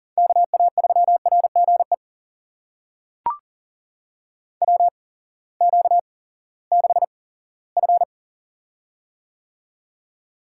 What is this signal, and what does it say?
Morse code, heard as a keyed tone, about 38 wpm, spaced as for 20 wpm, 700 Hz. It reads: KA3RGE WQ6F
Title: USA Call Signs - 8x Character Spacing - 40wpm